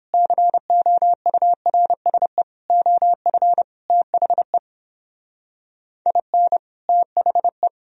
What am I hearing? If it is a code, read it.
Morse code, heard as a keyed tone, about 31 wpm, 700 Hz. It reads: COURSE OF THE IN THE